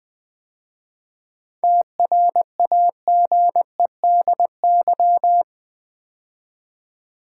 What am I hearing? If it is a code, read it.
Morse code, heard as a keyed tone, about 20 wpm, 700 Hz. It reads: TRAGEDY
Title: 7 Letter Words - First 3 Letters Given - 20wpm